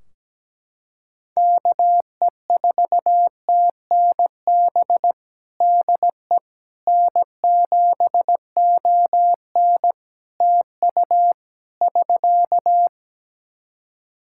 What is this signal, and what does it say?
Morse code, heard as a keyed tone, about 17 wpm, 700 Hz. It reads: KE4TNB DE N7ON TU <SK>
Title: Practice QSO - Exchange 4 - Ending - 17wpm